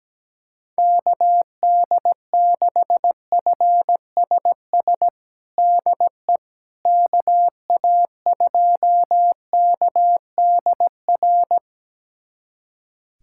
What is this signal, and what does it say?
Morse code, heard as a keyed tone, about 17 wpm, 700 Hz. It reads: KD6FSS DE KA2KDR
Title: USA Callsign DE Callsign 17wpm